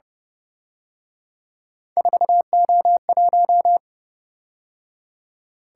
Morse code, heard as a keyed tone, about 28 wpm, 700 Hz. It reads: 4O1